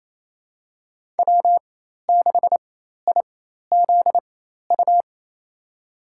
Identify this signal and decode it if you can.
Morse code, heard as a keyed tone, about 27 wpm, spaced as for 13 wpm, 700 Hz. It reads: W6IZU
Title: USA Call Signs - 5 Characters - 4x Character Spacing - 28wpm